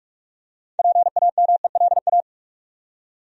Morse code, heard as a keyed tone, about 43 wpm, 700 Hz. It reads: PAMELA
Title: USA Common Names - 45wpm